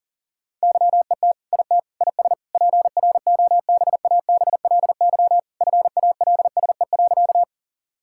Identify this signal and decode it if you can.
Morse code, heard as a keyed tone, about 40 wpm, 700 Hz. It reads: YET IT IS PROBABLY FALSE.
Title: Sentences from Top 1600 Words 40wpm